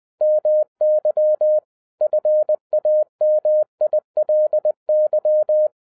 Morse code, heard as a keyed tone, about 20 wpm, 610 Hz. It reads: MY FAMILY
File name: 2011-07-mp3 / n58 BACK FROM A BIG ACCIDENT 20wpm